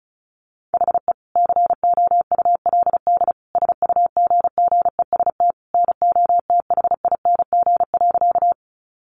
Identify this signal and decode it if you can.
Morse code, heard as a keyed tone, about 34 wpm, 700 Hz. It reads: HE COULD SUGGEST NOTHING.
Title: Sentences from Top 1300 Words 35wpm